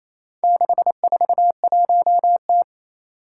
Morse code, heard as a keyed tone, about 28 wpm, 700 Hz. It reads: 641T
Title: Four Challenging Letter-Number Combinations 28wpm